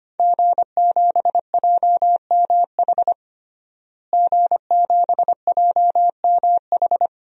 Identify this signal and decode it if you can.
Morse code, heard as a keyed tone, about 25 wpm, 700 Hz. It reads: G7JMH G7JMH
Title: DXCC-100 Call Signs - 5 Characters - Repeated 3x - 25wpm